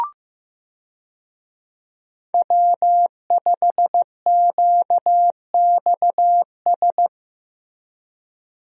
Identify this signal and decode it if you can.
Morse code, heard as a keyed tone, about 15 wpm, 700 Hz. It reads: W5QXS
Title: USA Call Signs - 5 Characters - Repeated 3x - 15wpm